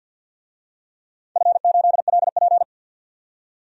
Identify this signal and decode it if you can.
Morse code, heard as a keyed tone, about 48 wpm, 700 Hz. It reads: F8LP
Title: DXCC-100 Call Signs - 4 Characters - 50wpm